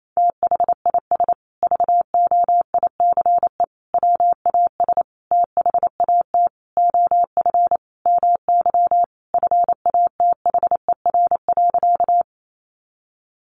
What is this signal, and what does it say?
Morse code, heard as a keyed tone, about 28 wpm, 700 Hz. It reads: THIS VOICE WAS THAT OF MY FATHER.
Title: Sentences from Top 500 Words 28wpm